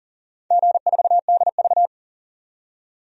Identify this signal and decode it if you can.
Morse code, heard as a keyed tone, about 40 wpm, 700 Hz. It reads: G4DV
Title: DXCC-100 Call Signs - 4 Characters - 40wpm